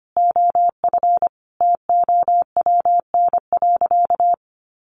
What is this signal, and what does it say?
Morse code, heard as a keyed tone, about 25 wpm, 700 Hz. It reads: OF TOWN.
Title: Sentences from Top 1500 Words 25wpm